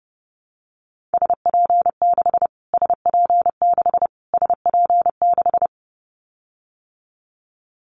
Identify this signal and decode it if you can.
Morse code, heard as a keyed tone, about 30 wpm, 700 Hz. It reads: SP6 SP6 SP6
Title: Callsign Prefixes for Top 25 DXCC - Repeated 3x - 30wpm